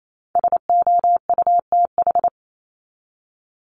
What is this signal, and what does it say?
Morse code, heard as a keyed tone, about 28 wpm, 700 Hz. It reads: SOUTH
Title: Top 500 Words 28wpm